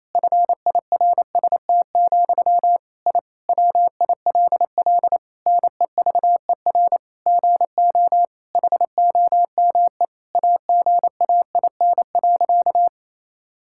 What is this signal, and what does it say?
Morse code, heard as a keyed tone, about 28 wpm, 700 Hz. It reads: FIRST, I WILL NEVER GO HOME AGAIN.
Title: Sentences from Top 1000 Words - Review Part 6 - 28wpm